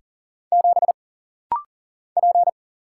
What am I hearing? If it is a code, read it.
Morse code, heard as a keyed tone, about 40 wpm, 700 Hz. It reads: 7 P
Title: Single Letter-Number Alt - Rapid-Fire 40wpm